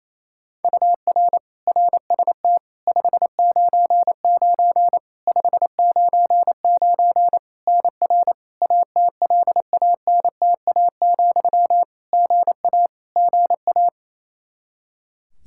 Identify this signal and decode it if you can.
Morse code, heard as a keyed tone, about 27 wpm, 700 Hz. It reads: UR RST 599 599 NR ATLANTA, GA GA
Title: Practice QSO - Exchange 1 - Beginning - 28wpm